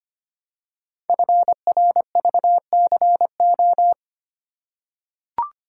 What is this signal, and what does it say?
Morse code, heard as a keyed tone, about 25 wpm, 700 Hz. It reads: FRVCO
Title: Code Groups - 5 Letters - 25wpm